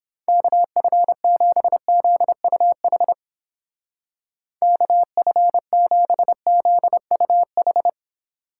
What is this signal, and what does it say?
Morse code, heard as a keyed tone, about 29 wpm, 700 Hz. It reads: KF7ZUH KF7ZUH